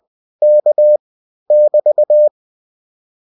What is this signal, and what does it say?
Morse code, heard as a keyed tone, about 21 wpm, 600 Hz. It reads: K <BT>